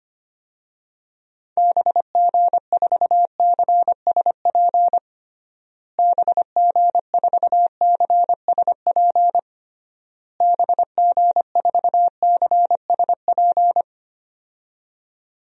Morse code, heard as a keyed tone, about 25 wpm, 700 Hz. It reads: BG4CSP BG4CSP BG4CSP
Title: DXCC-100 Call Signs - 6 Characters - Repeated 3x - 25wpm